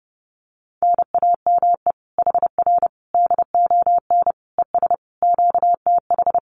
Morse code, heard as a keyed tone, about 28 wpm, 700 Hz. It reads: NAME HR DON ES QTH
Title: Common QSO Phrases Encore 30wpm